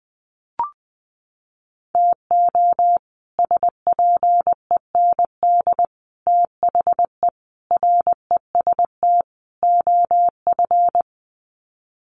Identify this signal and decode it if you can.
Morse code, heard as a keyed tone, about 20 wpm, 700 Hz. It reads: TO SPEND THE REST OF
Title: Sets of 5 Words Encore2 20wpm